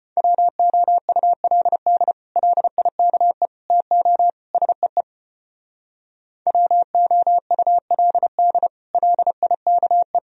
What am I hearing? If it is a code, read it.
Morse code, heard as a keyed tone, about 33 wpm, 700 Hz. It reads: WOULD LIKE TO SEE WOULD LIKE